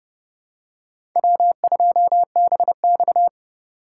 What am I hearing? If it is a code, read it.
Morse code, heard as a keyed tone, about 29 wpm, 700 Hz. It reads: W2BX